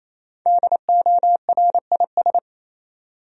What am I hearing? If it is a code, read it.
Morse code, heard as a keyed tone, about 28 wpm, 700 Hz. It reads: DORIS
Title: USA Common Names - 28wpm